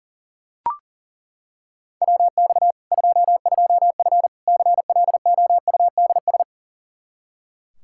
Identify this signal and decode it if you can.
Morse code, heard as a keyed tone, about 40 wpm, 700 Hz. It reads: WX 22F CLOUDS
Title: Practice QSO - Exchange 2 - Info - 40wpm